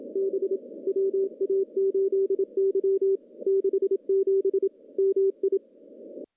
Morse code, heard as a keyed tone, about 24 wpm, 390 Hz. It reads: B WA8Y 67 MI
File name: contests / CW-SS-k5tr- -0200